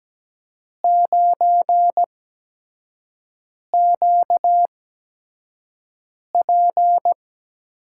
Morse code, heard as a keyed tone, about 17 wpm, spaced as for 5 wpm, 700 Hz. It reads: 9QP
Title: USA Call Signs - 5 Characters - 8x Character Spacing - 17wpm